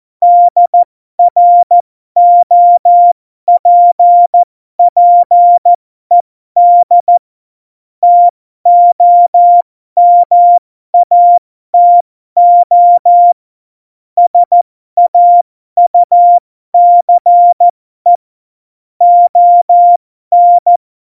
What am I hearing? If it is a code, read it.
Morse code, heard as a keyed tone, about 14 wpm, spaced as for 12 wpm, 700 Hz. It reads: DROPPED TOMATO SAUCE ON
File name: Morsekurs / Witze